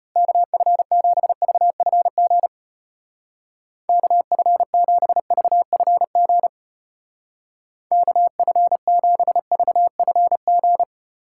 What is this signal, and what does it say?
Morse code, heard as a keyed tone, about 33 wpm, 700 Hz. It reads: KF7VFG KF7VFG KF7VFG